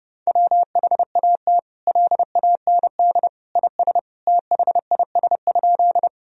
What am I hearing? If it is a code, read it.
Morse code, heard as a keyed tone, about 30 wpm, 700 Hz. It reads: WHAT LAND IS THIS?
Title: Sentences from Top 1000 Words - Review Part 1 - 30wpm